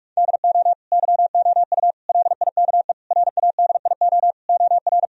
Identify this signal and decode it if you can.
Morse code, heard as a keyed tone, about 46 wpm, 700 Hz. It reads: DO YOU LIKE RADIO OR
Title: Sentences from Top 1000 Words - Review Part 4 - 45wpm